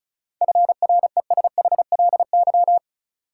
Morse code, heard as a keyed tone, about 34 wpm, 700 Hz. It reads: FRESHLY
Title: Sets of 5 Words Encore - New Word Drill - 35wpm